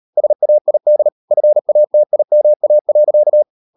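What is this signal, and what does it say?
Morse code, heard as a keyed tone, about 38 wpm, 600 Hz. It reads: SAID FATIMA.
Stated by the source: Morse code practice at 38 WPM, sending Ken Follett's novel Never in English, from CW med Gnister.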